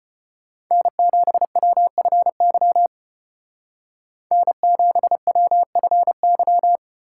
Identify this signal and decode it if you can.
Morse code, heard as a keyed tone, about 30 wpm, 700 Hz. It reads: N7WFY N7WFY
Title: Speed Racing - USA 1x3 Call Signs 25wpm